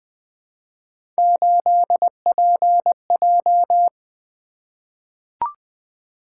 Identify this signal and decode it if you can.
Morse code, heard as a keyed tone, about 21 wpm, 700 Hz. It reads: 8PJ
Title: Three Letter-Number Combinations - 20wpm